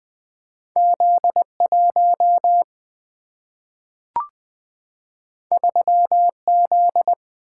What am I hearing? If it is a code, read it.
Morse code, heard as a keyed tone, about 20 wpm, 700 Hz. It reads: Z1 3Z
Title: Two Letter-Number Combinations - 20wpm